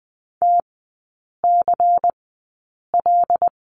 Morse code, heard as a keyed tone, about 20 wpm, spaced as for 8 wpm, 700 Hz. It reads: TCL
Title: 093 Lesson 16e - Letter c - Three Characters 20wpm